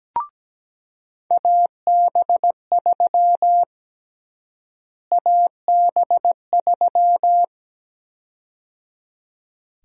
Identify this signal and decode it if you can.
Morse code, heard as a keyed tone, about 17 wpm, 700 Hz. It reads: AB3 AB3
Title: USA Call Sign Prefixes - Repeated 2x - 17wpm